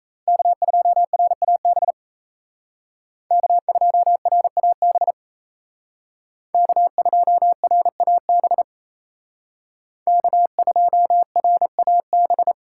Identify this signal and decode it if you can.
Morse code, heard as a keyed tone, about 36 wpm, 700 Hz. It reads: K2RAB K2RAB K2RAB K2RAB